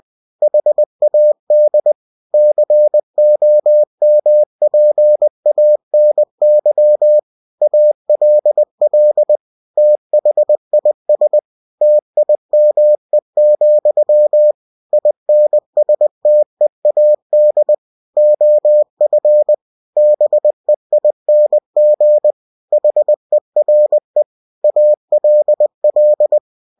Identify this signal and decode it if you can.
Morse code, heard as a keyed tone, about 20 wpm, 600 Hz. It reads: HAD COMPANY ALL THIS TIME, INSTEAD OF BEING HERE ALL
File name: cw-books / winnie-the-pooh / winnie-20wpm-page-0090